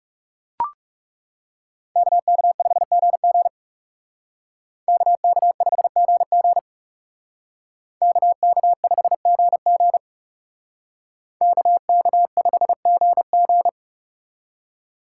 Morse code, heard as a keyed tone, about 39 wpm, 700 Hz. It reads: KK5GG KK5GG KK5GG KK5GG